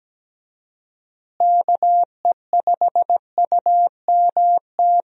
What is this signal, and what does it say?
Morse code, heard as a keyed tone, about 17 wpm, 700 Hz. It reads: KE5UMT
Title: USA Call Signs - 6 Characters - Repeated 2x - 17wpm